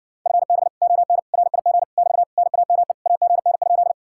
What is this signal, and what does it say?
Morse code, heard as a keyed tone, about 59 wpm, 700 Hz. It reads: FB ON RIG <BT> NAME AGN?